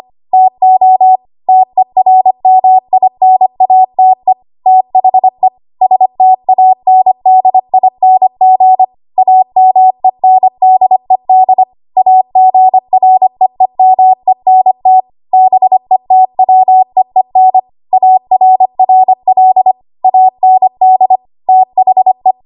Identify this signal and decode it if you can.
Morse code, heard as a keyed tone, about 25 wpm, 750 Hz. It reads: TO TERMINATE THE STANDING AMENDED AGREEMENT BETWEEN ARRL AND THE
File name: Archive / 25 WPM / 25WPM